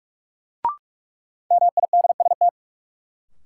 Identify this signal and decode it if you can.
Morse code, heard as a keyed tone, about 46 wpm, 700 Hz. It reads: MIDST
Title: Sets of 4 Words - New Word Drill - 45wpm